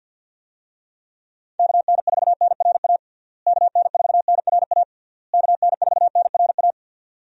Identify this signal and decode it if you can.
Morse code, heard as a keyed tone, about 52 wpm, 700 Hz. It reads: KN4NRA KN4NRA KN4NRA
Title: USA Call Signs - 6 Characters - Repeated 3x - 50wpm